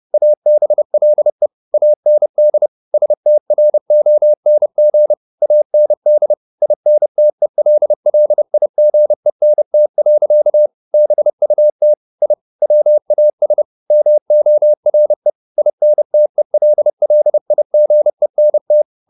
Morse code, heard as a keyed tone, about 30 wpm, 600 Hz. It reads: ABLE AND STRONG AND INTELLIGENT. BUT I WAS MORE INTELLIGENT